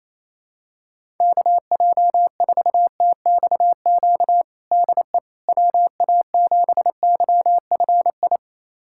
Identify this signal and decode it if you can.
Morse code, heard as a keyed tone, about 28 wpm, 700 Hz. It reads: KJ4TXQ DE WA7YFI
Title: USA Callsign DE Callsign - 6 Characters - 28wpm